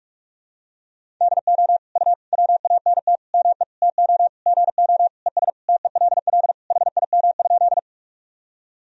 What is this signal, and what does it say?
Morse code, heard as a keyed tone, about 46 wpm, 700 Hz. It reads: DO U WANT ME TO GO ES TELL HIM?
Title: Sentences from Top 200 Words and Abbreviations 45wpm